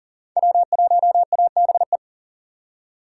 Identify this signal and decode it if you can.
Morse code, heard as a keyed tone, about 38 wpm, 700 Hz. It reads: W1ABE
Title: USA Call Signs - 6x Character Spacing - 40wpm